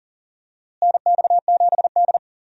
Morse code, heard as a keyed tone, about 40 wpm, 700 Hz. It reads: NX7D
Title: USA Call Signs - 4 Characters - Repeated 2x - 40wpm